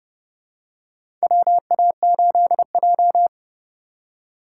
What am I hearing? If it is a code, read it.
Morse code, heard as a keyed tone, about 31 wpm, 700 Hz. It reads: WA8J
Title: USA Call Signs - 4 Characters - 3x Character Spacing - 30wpm